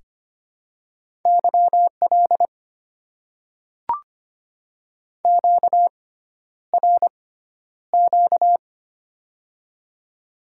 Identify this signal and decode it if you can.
Morse code, heard as a keyed tone, about 25 wpm, spaced as for 11 wpm, 700 Hz. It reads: YL QRQ
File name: Top 90 QSO Elements - 6x Character Spacing - 25wpm